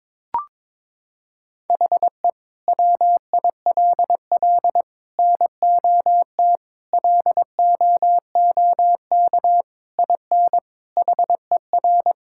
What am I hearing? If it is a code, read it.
Morse code, heard as a keyed tone, about 22 wpm, 700 Hz. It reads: HE WILL NOT LOOK IN HER
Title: Sentences from Top 1000 Words - Review Part 5 - 22wpm